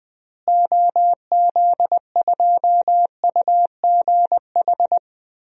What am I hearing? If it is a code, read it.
Morse code, heard as a keyed tone, about 21 wpm, 700 Hz. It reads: OZ2UGH
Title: DXCC-100 Call Signs - 6 Characters - Repeated 3x - 20wpm